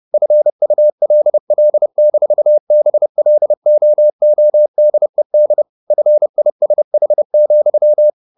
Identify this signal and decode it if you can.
Morse code, heard as a keyed tone, about 30 wpm, 600 Hz. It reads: FULL-BLOODED FISH,
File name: the old man and the sea / the old man and the sea 1 30wpm